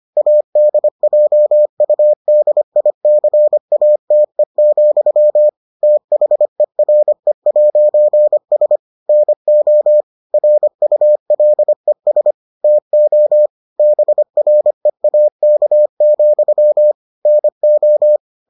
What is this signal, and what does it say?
Morse code, heard as a keyed tone, about 25 wpm, 600 Hz. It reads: ADJUDICATE, THERE'S NO RULES TO BREAK, NO